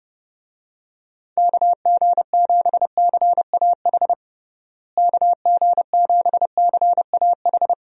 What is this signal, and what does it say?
Morse code, heard as a keyed tone, about 30 wpm, 700 Hz. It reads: KG7CAH KG7CAH
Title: USA Call Signs - 6 Characters - Repeated 2x - 30wpm